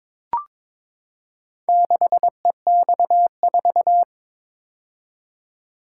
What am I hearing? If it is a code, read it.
Morse code, heard as a keyed tone, about 21 wpm, 700 Hz. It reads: 6EX4
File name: practice / 22wpm / Four Letter-Number Combinations - 22wpm